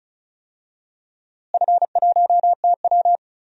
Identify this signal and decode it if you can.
Morse code, heard as a keyed tone, about 36 wpm, 700 Hz. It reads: F1TW
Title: DXCC-100 Call Signs - 4 Characters - Repeated 2x - 35wpm